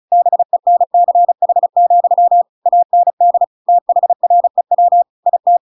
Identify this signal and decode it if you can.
Morse code, heard as a keyed tone, about 35 wpm, 700 Hz. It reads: BENCH, AND THREW IT